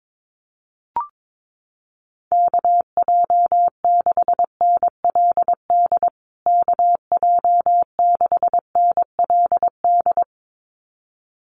Morse code, heard as a keyed tone, about 22 wpm, 700 Hz. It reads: KJ6NLD KJ6NLD
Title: USA Call Signs Repeated 2x 22wpm